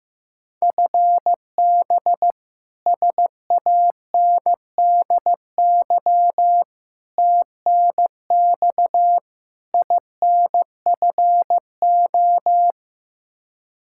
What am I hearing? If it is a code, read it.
Morse code, heard as a keyed tone, about 15 wpm, 700 Hz. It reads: FB SANDY TNX INFO
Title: Practice QSO - Exchange 3 - More Info - 15wpm